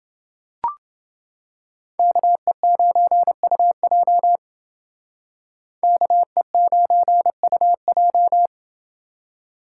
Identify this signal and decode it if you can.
Morse code, heard as a keyed tone, about 28 wpm, 700 Hz. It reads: KE9UJ KE9UJ